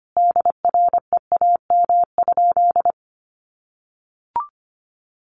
Morse code, heard as a keyed tone, about 25 wpm, 700 Hz. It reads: DREAM?
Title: Sentences from Top 1500 Words 25wpm